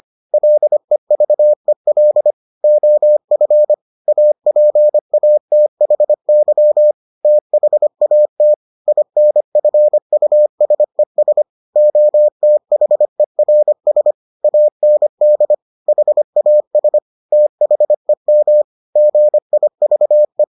Morse code, heard as a keyed tone, about 25 wpm, 600 Hz. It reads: LEVEL OF APATHY THAT INFUSES OTHERS AND HAS THEM GIVE